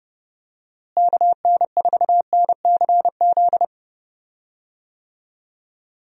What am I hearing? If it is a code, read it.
Morse code, heard as a keyed tone, about 30 wpm, 700 Hz. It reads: KN4NCZ
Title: USA Call Signs - 6 Characters - Repeated 3x - 30wpm